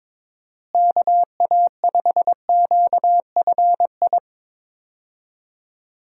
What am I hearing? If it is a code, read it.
Morse code, heard as a keyed tone, about 22 wpm, 700 Hz. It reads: KA5QFI
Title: USA Call Signs - 6 Characters - Repeated 3x - 22wpm